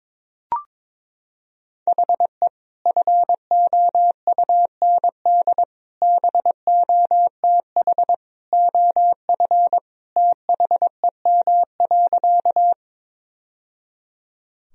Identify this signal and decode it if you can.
Morse code, heard as a keyed tone, about 22 wpm, 700 Hz. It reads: HE FOUND BOTH OF THEM.
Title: Sentences from Top 1000 Words - Review Part 1 - 22wpm